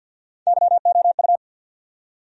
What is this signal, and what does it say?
Morse code, heard as a keyed tone, about 49 wpm, 700 Hz. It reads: YOU
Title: Top 100 Words Repeated 50wpm